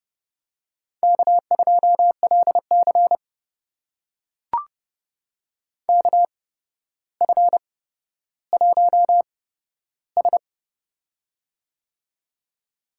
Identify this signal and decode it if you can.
Morse code, heard as a keyed tone, about 30 wpm, spaced as for 13 wpm, 700 Hz. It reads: K2LC KF1S
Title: USA Call Signs - 4 Characters - 8x Character Spacing - 30wpm